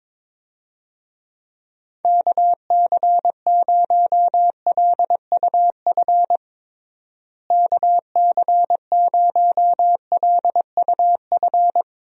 Morse code, heard as a keyed tone, about 21 wpm, 700 Hz. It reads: KC0LUF KC0LUF
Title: USA Call Signs - 6 Characters - Repeated 2x - 22wpm